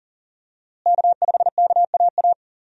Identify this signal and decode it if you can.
Morse code, heard as a keyed tone, about 40 wpm, 700 Hz. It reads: K5KAA
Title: USA Call Signs - 5 Characters - Repeated 2x - 40wpm